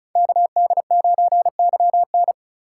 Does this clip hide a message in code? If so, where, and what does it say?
Yes: Morse code from 0.2 to 2.3 s, heard as a keyed tone, about 46 wpm, 700 Hz. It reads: KD9YN